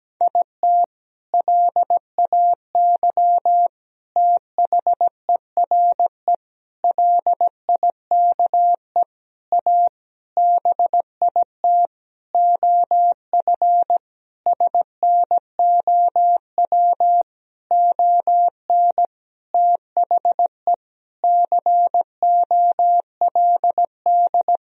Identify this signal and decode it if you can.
Morse code, heard as a keyed tone, about 17 wpm, 700 Hz. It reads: IT LAY THERE LIKE A BIT OF SNOW ON THE COLD
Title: Sentences from Top 1000 Words - Review Part 3 - 17wpm